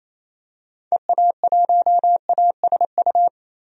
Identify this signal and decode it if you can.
Morse code, heard as a keyed tone, about 28 wpm, 700 Hz. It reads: EA1ASU
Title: DXCC-100 Call Signs - 6 Characters - 28wpm